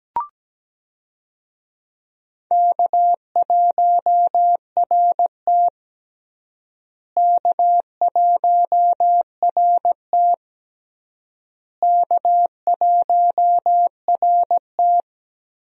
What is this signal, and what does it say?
Morse code, heard as a keyed tone, about 17 wpm, 700 Hz. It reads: K1RT K1RT K1RT
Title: USA Call Signs - 4 Characters - Repeated 3x - 17wpm